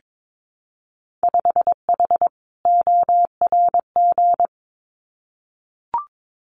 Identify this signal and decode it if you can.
Morse code, heard as a keyed tone, about 22 wpm, 700 Hz. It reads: 5H ORG
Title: Winter Field Day Exchanges Encore 22wpm